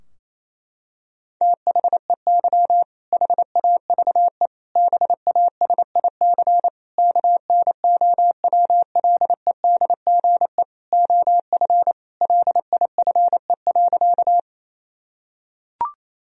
Morse code, heard as a keyed tone, about 28 wpm, 700 Hz. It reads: THEY HAVE BASIC KNOWLEDGE OF LIFE.
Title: Sentences from Top 1000 Words - Review Part 3 - 28wpm